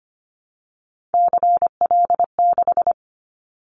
Morse code, heard as a keyed tone, about 25 wpm, 700 Hz. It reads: CL6
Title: Callsign Prefixes for Top 100 DXCC - 25wpm